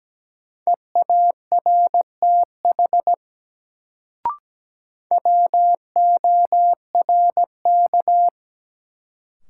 Morse code, heard as a keyed tone, about 17 wpm, 700 Hz. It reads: EARTH WORK
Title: Sentences from Top 200 Words - New Word Drill - 17wpm